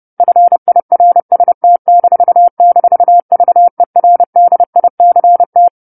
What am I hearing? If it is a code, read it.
Morse code, heard as a keyed tone, about 30 wpm, 700 Hz. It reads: FIRST--VERDICT
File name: 10-Chapter-10 30 WPM